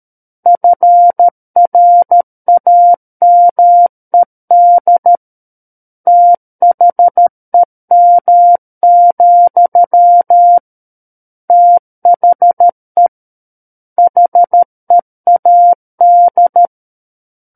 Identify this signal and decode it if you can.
Morse code, heard as a keyed tone, about 13 wpm, 700 Hz. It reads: FRAMED THEM, THE HEAD
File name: Book One / 04-Chapter 4 - 13 WPM